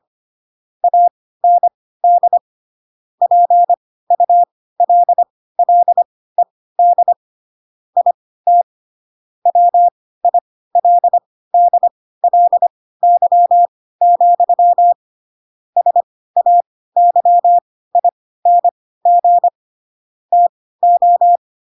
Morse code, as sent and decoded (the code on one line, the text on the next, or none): .- -. -.. / .--. ..- .-.. .-.. -.. / .. - / .-- .. .-.. -.. .-.. -.-- --..-- / ... .- -.-- .. -. --. / - ---
AND PULLED IT WILDLY, SAYING TO